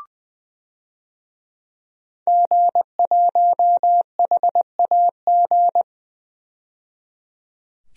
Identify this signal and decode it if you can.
Morse code, heard as a keyed tone, about 20 wpm, 700 Hz. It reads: G1HAG
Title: DXCC-100 Call Signs - 5 Characters - 20wpm